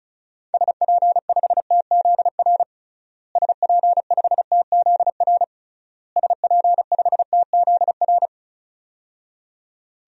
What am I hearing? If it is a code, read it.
Morse code, heard as a keyed tone, about 36 wpm, 700 Hz. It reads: SP5TZR SP5TZR SP5TZR
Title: DXCC-100 Call Signs - 6 Characters - Repeated 3x - 35wpm